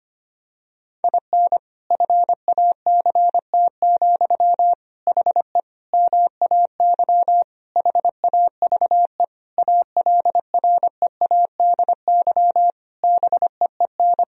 Morse code, heard as a keyed tone, about 25 wpm, 700 Hz. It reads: IN FACT, HE MAY HAVE ALREADY BEEN
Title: Sentences from Top 1000 Words - Review Part 3 - 25wpm